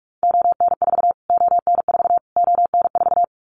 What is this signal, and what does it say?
Morse code, heard as a keyed tone, about 44 wpm, 700 Hz. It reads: ON4 ON4 ON4
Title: Callsign Prefixes for Top 25 DXCC - Repeated 3x - 45wpm